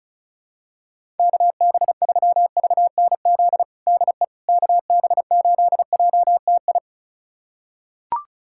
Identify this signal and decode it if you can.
Morse code, heard as a keyed tone, about 34 wpm, 700 Hz. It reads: KB3VNZ DE KB8JTI
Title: USA Callsign DE Callsign 35wpm